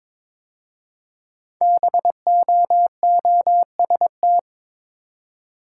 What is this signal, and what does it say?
Morse code, heard as a keyed tone, about 23 wpm, 700 Hz. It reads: BOOST
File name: 5 Letter Words - First 3 Letters Given - 22wpm